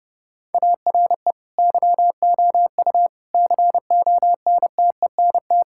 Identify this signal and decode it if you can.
Morse code, heard as a keyed tone, about 28 wpm, 700 Hz. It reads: ARE YOU CONTENT
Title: Sentences from Top 1000 Words - Review Part 6 - 30wpm